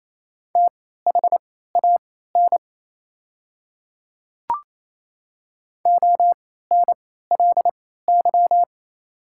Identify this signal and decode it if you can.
Morse code, heard as a keyed tone, about 28 wpm, spaced as for 16 wpm, 700 Hz. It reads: THAN ONLY
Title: 3x Character Spacing - Top 200 Words - 28wpm